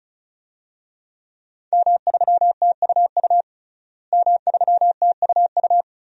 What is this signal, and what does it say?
Morse code, heard as a keyed tone, about 36 wpm, 700 Hz. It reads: M3TUU M3TUU
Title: DXCC-100 Call Signs - 5 Characters - Repeated 2x - 35wpm